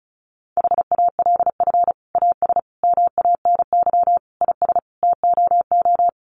Morse code, heard as a keyed tone, about 34 wpm, 700 Hz. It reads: HALF AS MANY IS TOO
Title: Sentences from Top 400 Words 35wpm